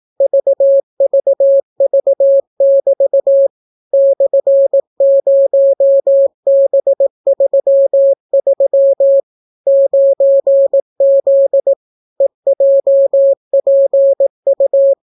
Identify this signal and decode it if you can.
Morse code, heard as a keyed tone, about 18 wpm, 550 Hz. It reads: VVV<BT> /0B33 9Z EJPU